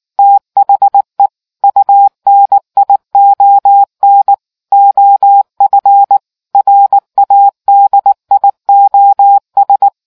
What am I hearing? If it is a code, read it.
Morse code, heard as a keyed tone, about 19 wpm, 800 Hz. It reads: THE UNION OF RADIOS